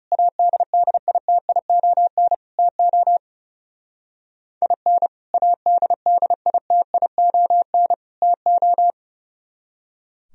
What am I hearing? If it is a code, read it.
Morse code, heard as a keyed tone, about 37 wpm, 700 Hz. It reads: ADDITION TO IN ADDITION TO
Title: Speed-Racing - Sets of 3 Words 30wpm